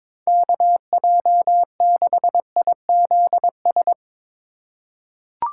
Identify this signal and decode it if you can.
Morse code, heard as a keyed tone, about 22 wpm, 700 Hz. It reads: KJ6IZS
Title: USA Call Signs - 3x Character Spacing - 22wpm